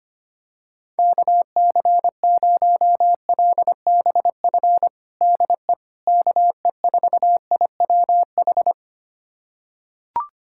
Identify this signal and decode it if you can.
Morse code, heard as a keyed tone, about 25 wpm, 700 Hz. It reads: KC0LBF DE KE4IWH
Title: USA Callsign DE Callsign - 6 Characters - 25wpm